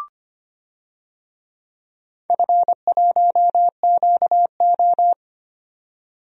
Morse code, heard as a keyed tone, about 25 wpm, 690 Hz. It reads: F1QO
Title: DXCC-100 Call Signs - 4 Characters - 25wpm